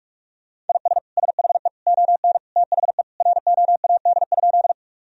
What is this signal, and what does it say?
Morse code, heard as a keyed tone, about 46 wpm, 700 Hz. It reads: IS SHE ON THE ROAD?
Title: Sentences from Top 1000 Words - Review Part 2 - 45wpm